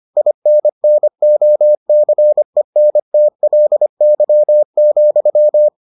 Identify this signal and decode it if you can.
Morse code, heard as a keyed tone, about 25 wpm, 600 Hz. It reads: INNOCENTLY,